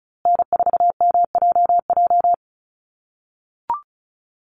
Morse code, heard as a keyed tone, about 34 wpm, 700 Hz. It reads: N4MJJ
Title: USA Call Signs Repeated 3x 35wpm